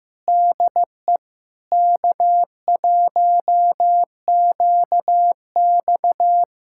Morse code, heard as a keyed tone, about 15 wpm, 700 Hz. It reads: DE K1QX
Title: Practice QSO - Exchange 4 - Ending - 15wpm